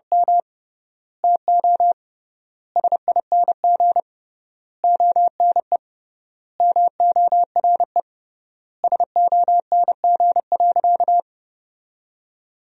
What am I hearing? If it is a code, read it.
Morse code, heard as a keyed tone, about 30 wpm, 700 Hz. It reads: M TO SING ONE MORE SONG.
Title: 3x Word-Spacing - Sentences from Top 400 Words - 30wpm